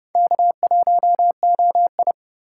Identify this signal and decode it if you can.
Morse code, heard as a keyed tone, about 30 wpm, 700 Hz. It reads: K1OI
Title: USA Call Signs - 8x Character Spacing - 30wpm